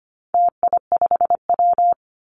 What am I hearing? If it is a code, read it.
Morse code, heard as a keyed tone, about 25 wpm, 700 Hz. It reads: TI5W
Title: Call Signs - 25wpm